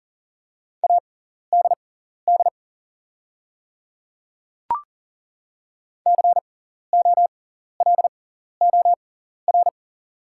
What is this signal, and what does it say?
Morse code, heard as a keyed tone, about 39 wpm, spaced as for 14 wpm, 690 Hz. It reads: ADD COLOR
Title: 6x Character Spacing - Top 500 Words - 40wpm